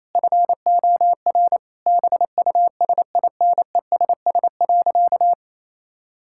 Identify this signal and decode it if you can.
Morse code, heard as a keyed tone, about 28 wpm, 700 Hz. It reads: FOR BUSINESS.
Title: Sentences from Top 1000 Words - Review Part 5 - 28wpm